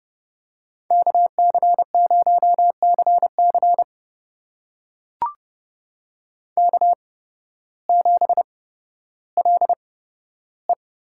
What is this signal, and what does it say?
Morse code, heard as a keyed tone, about 30 wpm, spaced as for 14 wpm, 700 Hz. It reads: KC0CC K7LE
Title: USA Call Signs - 5 Characters - 8x Character Spacing - 30wpm